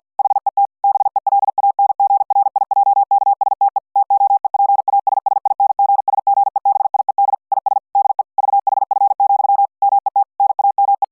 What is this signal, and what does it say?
Morse code, heard as a keyed tone, about 50 wpm, 800 Hz. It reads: HET BELANGRIJKSTE TOEPASSINGSGEBIED IS DE VHF, MET NAME